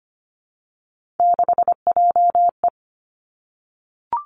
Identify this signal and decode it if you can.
Morse code, heard as a keyed tone, about 25 wpm, 700 Hz. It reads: 6JE
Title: Three Challenging Letter-Number Combinations 25wpm